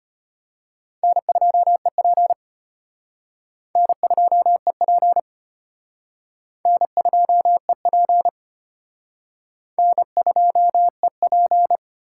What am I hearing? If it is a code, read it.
Morse code, heard as a keyed tone, about 34 wpm, 700 Hz. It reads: N2EP N2EP N2EP N2EP